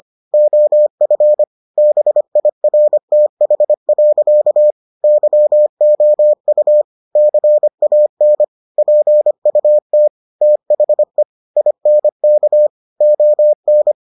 Morse code, heard as a keyed tone, about 25 wpm, 600 Hz. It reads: OF BIRTH. YOU CAN PUT THE INK ON